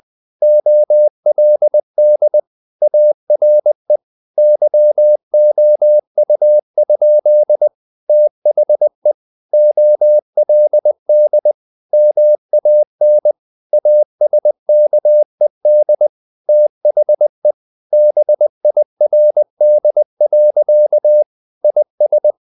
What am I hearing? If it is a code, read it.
Morse code, heard as a keyed tone, about 20 wpm, 600 Hz. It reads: OLD ARE YOU? THE OLD MAN ASKED THE BIRD. IS
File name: the old man and the sea / the old man and the sea part 1 20wpm